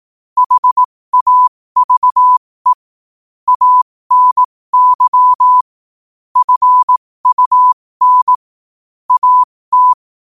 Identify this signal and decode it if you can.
Morse code, heard as a keyed tone, about 18 wpm, spaced as for 15 wpm, 1000 Hz. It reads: HAVE ANY FUN AT